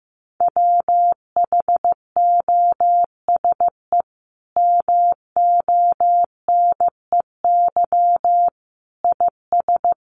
Morse code, heard as a keyed tone, about 15 wpm, 700 Hz. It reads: WHOSE MONEY IS
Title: Sentences from Top 1200 Words 15wpm